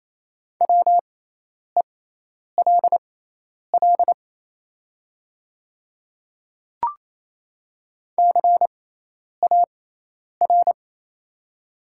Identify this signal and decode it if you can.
Morse code, heard as a keyed tone, about 27 wpm, spaced as for 10 wpm, 700 Hz. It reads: WELL CAR
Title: 6x Character Spacing - Top 300 Words - 28wpm